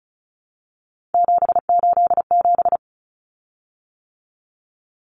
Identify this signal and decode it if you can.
Morse code, heard as a keyed tone, about 34 wpm, 700 Hz. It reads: 787
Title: Common Cut Numbers - 3-Digit 35wpm